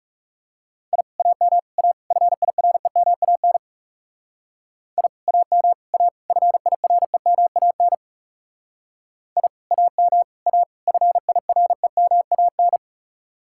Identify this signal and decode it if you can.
Morse code, heard as a keyed tone, about 35 wpm, 700 Hz. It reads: I AM A FIREMAN I AM A FIREMAN I AM A FIREMAN
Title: Speed-Racing Common QSO Phrases - 30wpm